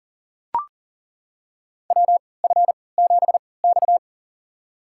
Morse code, heard as a keyed tone, about 40 wpm, spaced as for 23 wpm, 700 Hz. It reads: WF7X
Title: USA Call Signs - 4 Characters - 3x Character Spacing - 40wpm